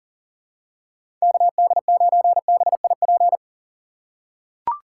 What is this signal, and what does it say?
Morse code, heard as a keyed tone, about 37 wpm, 700 Hz. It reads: KD9BIP
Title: USA Call Signs - 6 Characters - 2x Character Spacing - 40wpm